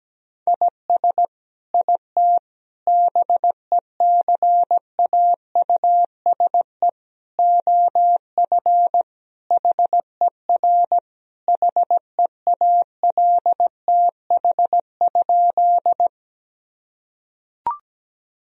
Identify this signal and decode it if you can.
Morse code, heard as a keyed tone, about 17 wpm, 700 Hz. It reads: IS IT BECAUSE OF HER HEALTH?
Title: Sentences from Top 1000 Words - Review Part 3 - 17wpm